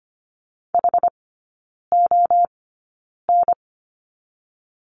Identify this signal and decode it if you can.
Morse code, heard as a keyed tone, about 25 wpm, spaced as for 9 wpm, 690 Hz. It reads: HON
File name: Lesson 14e - Number 2 - Three Characters 25wpm